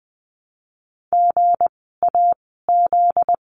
Morse code, heard as a keyed tone, about 20 wpm, 700 Hz. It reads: G A Z
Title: 244 Lesson 39f - Letter z - Words 20wpm